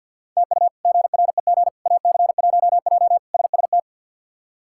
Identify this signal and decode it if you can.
Morse code, heard as a keyed tone, about 52 wpm, 700 Hz. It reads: TU GREG AC1J SST
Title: K1USN SST QSO Practice - 50wpm